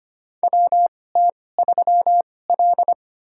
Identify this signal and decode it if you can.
Morse code, heard as a keyed tone, about 25 wpm, spaced as for 18 wpm, 700 Hz. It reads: WT3L
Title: USA Call Signs - 4 Characters - 2x Character Spacing - 25wpm